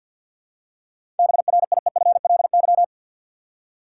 Morse code, heard as a keyed tone, about 53 wpm, 700 Hz. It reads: BRIEFLY
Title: Seven Letter Words - 50wpm